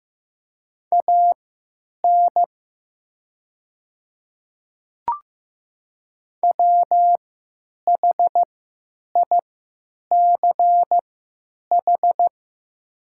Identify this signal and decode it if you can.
Morse code, heard as a keyed tone, about 15 wpm, spaced as for 9 wpm, 700 Hz. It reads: AN WHICH
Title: 3x Character Spacing - Top 100 Words - 15wpm